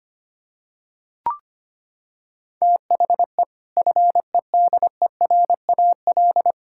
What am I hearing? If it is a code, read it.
Morse code, heard as a keyed tone, about 25 wpm, 700 Hz. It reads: THE FEDERAL
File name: practice / 25wpm / Sentences with 3-Word Phrases - 25wpm